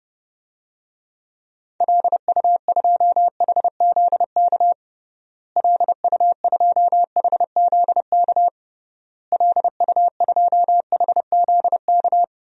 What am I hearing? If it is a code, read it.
Morse code, heard as a keyed tone, about 30 wpm, 700 Hz. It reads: LU2HZK LU2HZK LU2HZK
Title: DXCC-100 Call Signs - 6 Characters - Repeated 3x - 30wpm